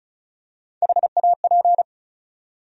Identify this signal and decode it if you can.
Morse code, heard as a keyed tone, about 36 wpm, 700 Hz. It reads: HAP
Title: USA Call Sign Postfixes - 3 Letters 35wpm